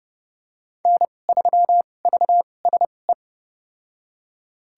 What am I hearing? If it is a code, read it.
Morse code, heard as a keyed tone, about 30 wpm, spaced as for 22 wpm, 690 Hz. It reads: N3VSE
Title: USA Call Signs - 5 Characters - 2x Character Spacing - 30wpm